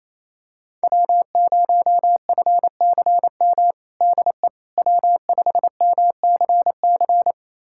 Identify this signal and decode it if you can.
Morse code, heard as a keyed tone, about 28 wpm, 700 Hz. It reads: W0FCM DE W5MCC
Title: USA Callsign DE Callsign - 5 Characters - 28wpm